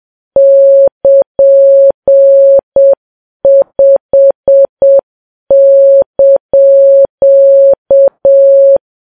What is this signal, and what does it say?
Morse code, heard as a keyed tone, about 7 wpm, 550 Hz. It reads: <KN>5)
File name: Files Web tls / Morse on9fv / morse7kps